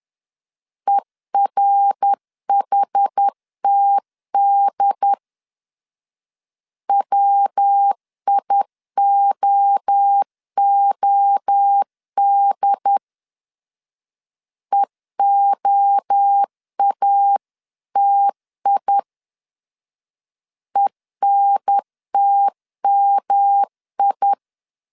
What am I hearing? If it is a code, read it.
Morse code, heard as a keyed tone, about 10 wpm, 800 Hz. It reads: ERHTD WIOOD EOATI ENTMI
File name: CW / DET / 1 3 1500C S C 10wpm 3335